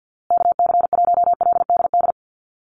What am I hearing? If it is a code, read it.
Morse code, heard as a keyed tone, about 55 wpm, 700 Hz. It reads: KC1LDD
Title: USA Call Signs Repeated 2x 50wpm